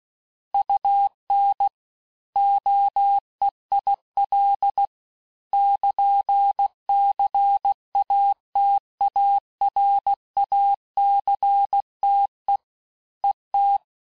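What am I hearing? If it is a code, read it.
Morse code, heard as a keyed tone, about 16 wpm, 800 Hz. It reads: UN OEIL <KN>CATARACTE ET
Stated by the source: Morse course text, end of QSO, sent at 15 words per minute.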